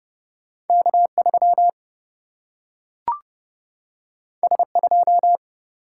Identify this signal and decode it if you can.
Morse code, heard as a keyed tone, about 30 wpm, 700 Hz. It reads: K 3 S 2
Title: Single-Letter Single-Number 30wpm